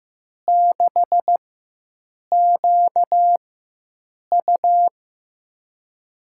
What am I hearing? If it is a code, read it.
Morse code, heard as a keyed tone, about 15 wpm, 700 Hz. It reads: 6 Q U
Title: USA Call Signs - 4 Characters - 4x Character Spacing - 15wpm